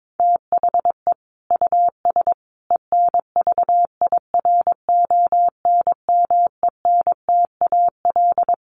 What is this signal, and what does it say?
Morse code, heard as a keyed tone, about 22 wpm, 700 Hz. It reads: THE US ENVIRONMENTAL
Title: Sets of 5 Words Encore2 22wpm